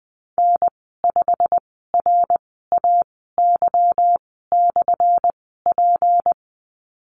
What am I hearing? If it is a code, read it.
Morse code, heard as a keyed tone, about 19 wpm, spaced as for 14 wpm, 700 Hz. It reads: N5RAY/P
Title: 179 Lesson 29g - Slash - Call Signs 20wpm